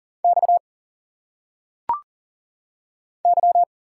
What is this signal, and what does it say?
Morse code, heard as a keyed tone, about 40 wpm, 700 Hz. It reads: X Y
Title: Single Letters - Phonetic Pronunciation 40wpm